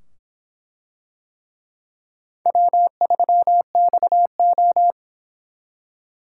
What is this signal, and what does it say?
Morse code, heard as a keyed tone, about 24 wpm, 700 Hz. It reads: W3XO